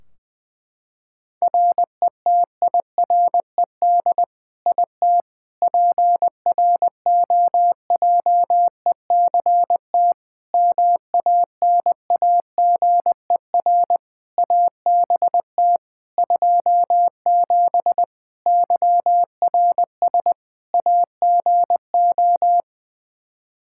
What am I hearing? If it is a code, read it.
Morse code, heard as a keyed tone, about 20 wpm, 700 Hz. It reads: RETIRED IT PROJECT MANAGER ABT 27 YRS AGO
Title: Practice QSO - Exchange 3 - More Info - 20wpm